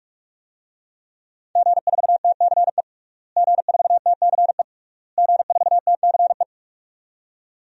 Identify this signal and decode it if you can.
Morse code, heard as a keyed tone, about 46 wpm, 700 Hz. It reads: G4TCE G4TCE G4TCE
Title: DXCC-100 Call Signs - 5 Characters - Repeated 3x - 45wpm